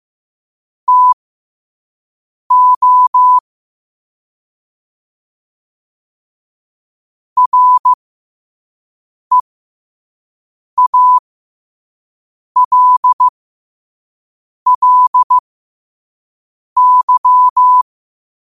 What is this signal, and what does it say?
Morse code, heard as a keyed tone, about 15 wpm, spaced as for 5 wpm, 1000 Hz. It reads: TO REALLY
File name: qotdfiles / 5wpm